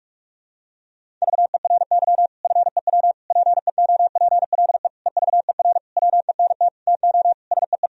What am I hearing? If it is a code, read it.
Morse code, heard as a keyed tone, about 46 wpm, 700 Hz. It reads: VERY FEW PEOPLE EVER WENT TO SEE
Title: Sentences from Top 1000 Words - Review Part 1 - 45wpm